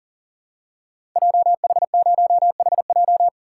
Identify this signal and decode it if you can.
Morse code, heard as a keyed tone, about 40 wpm, 700 Hz. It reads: JH0HJ
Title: Code Groups - 5 Letter:Numbers - 40wpm